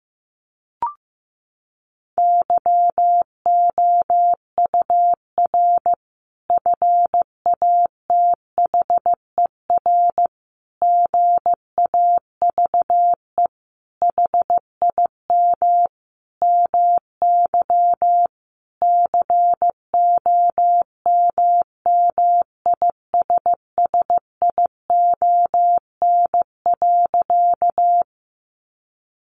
Morse code, heard as a keyed tone, about 15 wpm, 700 Hz. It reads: YOUR FATHER GAVE HIM MY COMMISSION.
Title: Sentences from Top 1400 Words 15wpm